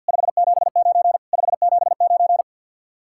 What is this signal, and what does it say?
Morse code, heard as a keyed tone, about 50 wpm, 700 Hz. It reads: 579 579
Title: Speed-Racing Common QSO Phrases Encore - 50wpm